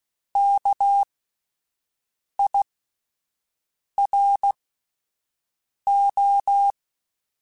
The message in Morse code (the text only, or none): KIRO